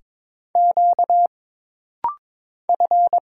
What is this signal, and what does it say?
Morse code, heard as a keyed tone, about 22 wpm, 700 Hz. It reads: Q F
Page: This rapid-fire timing is geared to push your ICR (Instant Character Recognition) skills.